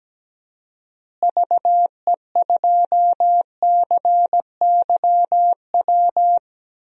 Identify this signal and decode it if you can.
Morse code, heard as a keyed tone, about 17 wpm, 700 Hz. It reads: VE2CYW
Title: DXCC-100 Call Signs - 6 Characters - Repeated 3x - 17wpm